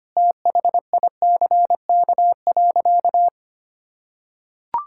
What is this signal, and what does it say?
Morse code, heard as a keyed tone, about 25 wpm, 700 Hz. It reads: THICK.
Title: Sentences from Top 1600 Words 25wpm